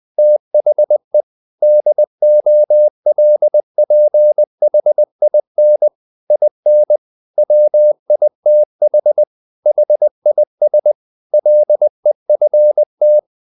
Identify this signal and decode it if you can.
Morse code, heard as a keyed tone, about 20 wpm, 600 Hz. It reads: THE DOLPHIN IN WITH HIS LEFT
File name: the old man and the sea part 2 20wpm